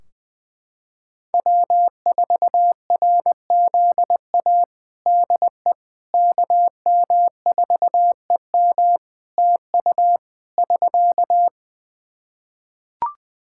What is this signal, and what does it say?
Morse code, heard as a keyed tone, about 20 wpm, 700 Hz. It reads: W4RZA DE KM4EM TU <SK>
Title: Practice QSO - Exchange 4 - Ending - 20wpm